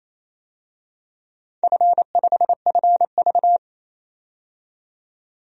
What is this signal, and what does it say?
Morse code, heard as a keyed tone, about 27 wpm, 690 Hz. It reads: F5FV
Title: DXCC-100 Call Signs - 4 Characters - 28wpm